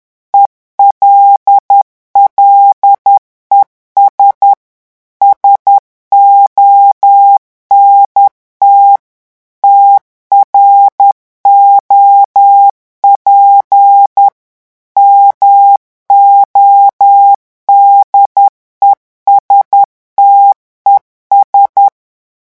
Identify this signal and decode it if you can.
Morse code, heard as a keyed tone, about 10 wpm, 800 Hz. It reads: ELLES SONT TROP MODESTES
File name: morse mp3 / revision9 citations humoristiques vitesse 10 mots minute